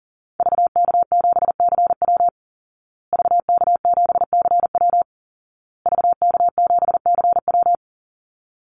Morse code, heard as a keyed tone, about 40 wpm, 700 Hz. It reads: VK7CW VK7CW VK7CW
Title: Call Signs Repeated 2x - 40wpm